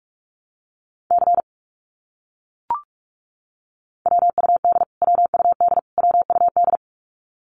Morse code, heard as a keyed tone, about 44 wpm, 700 Hz. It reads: C WUD WUD WUD
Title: Abbreviations Repeated 3x - 45wpm